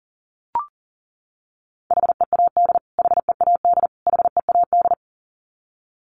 Morse code, heard as a keyed tone, about 40 wpm, 700 Hz. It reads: HEAD HEAD HEAD
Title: Top 500 Words Repeated 40wpm